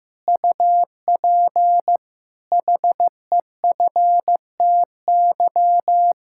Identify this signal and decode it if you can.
Morse code, heard as a keyed tone, about 16 wpm, 700 Hz. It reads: UP HEFTY
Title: Pangram Challenge v2 - 15wpm